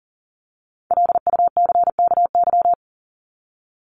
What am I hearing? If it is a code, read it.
Morse code, heard as a keyed tone, about 37 wpm, 700 Hz. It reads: LUCKY
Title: Top 1751-2000 Words 40wpm